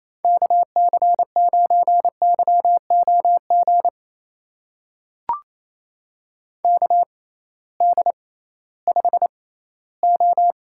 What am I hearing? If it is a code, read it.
Morse code, heard as a keyed tone, about 28 wpm, spaced as for 16 wpm, 700 Hz. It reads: KC9YOG KD5O
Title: USA Call Signs - 6 Characters - 6x Character Spacing - 28wpm